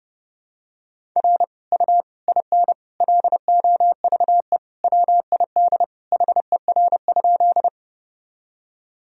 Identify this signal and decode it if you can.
Morse code, heard as a keyed tone, about 32 wpm, 700 Hz. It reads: R U IN LOVE WID HER?
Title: Sentences from Top 400 Words and Abbreviations 30wpm